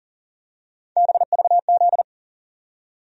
Morse code, heard as a keyed tone, about 40 wpm, 700 Hz. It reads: BVZ
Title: USA Call Sign Postfixes - 3 Letters 40wpm